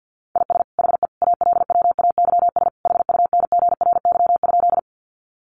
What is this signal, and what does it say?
Morse code, heard as a keyed tone, about 48 wpm, 700 Hz. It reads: IS HE ALWAYS HUNGRY?
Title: Sentences from Top 1400 Words 50wpm